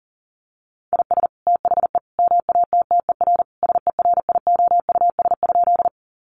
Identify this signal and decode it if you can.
Morse code, heard as a keyed tone, about 40 wpm, 700 Hz. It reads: IS THE MATTER SERIOUS?
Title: Sentences from Top 1100 Words 40wpm